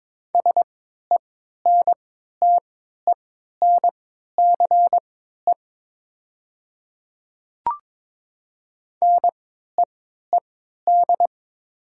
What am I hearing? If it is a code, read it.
Morse code, heard as a keyed tone, about 22 wpm, spaced as for 12 wpm, 700 Hz. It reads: SENTENCE NEED
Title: 3x Character Spacing - Top 200 Words - 22wpm